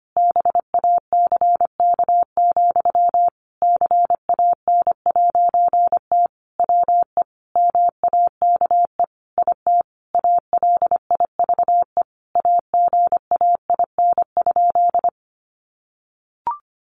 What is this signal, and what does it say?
Morse code, heard as a keyed tone, about 25 wpm, 700 Hz. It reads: BACK, CAN'T WE MAKE IT ALIVE AGAIN?
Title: A Sound Of Thunder 25wpm